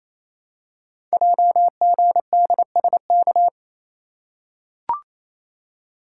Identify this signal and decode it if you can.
Morse code, heard as a keyed tone, about 28 wpm, 700 Hz. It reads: JGDSK
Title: Code Groups - 5 Letter:Numbers - 28wpm